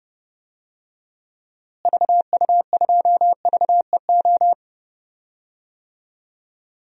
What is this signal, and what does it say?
Morse code, heard as a keyed tone, about 30 wpm, 700 Hz. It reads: VU2VEO
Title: DXCC-100 Call Signs - 6 Characters - Repeated 2x - 30wpm